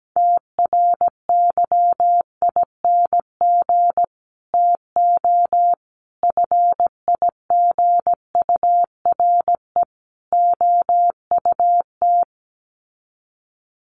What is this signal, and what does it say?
Morse code, heard as a keyed tone, about 17 wpm, 700 Hz. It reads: TRYING TO FIGURE OUT
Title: Sets of 4 Words 17wpm